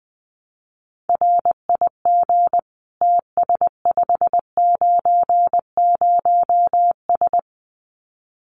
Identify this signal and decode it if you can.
Morse code, heard as a keyed tone, about 20 wpm, 700 Hz. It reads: RIG TS590S
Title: Common QSO Phrases Encore 20wpm